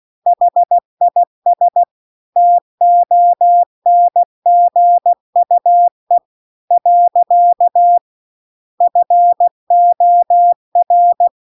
Morse code, heard as a keyed tone, about 16 wpm, 700 Hz. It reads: HIS TONGUE . FOR